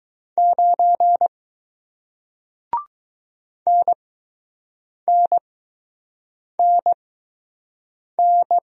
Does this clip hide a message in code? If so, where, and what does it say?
Yes: Morse code from 0.4 to 8.6 s, heard as a keyed tone, about 20 wpm, spaced as for 6 wpm, 700 Hz. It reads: 9 NNNN